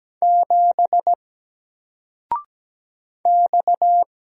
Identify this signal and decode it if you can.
Morse code, heard as a keyed tone, about 17 wpm, 700 Hz. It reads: 7 X
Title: Single Letter-Number 17wpm